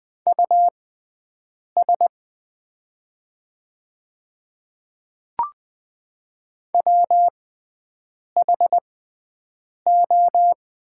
Morse code, heard as a keyed tone, about 20 wpm, spaced as for 7 wpm, 690 Hz. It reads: US WHO
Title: 6x Character Spacing - Top 300 Words - 20wpm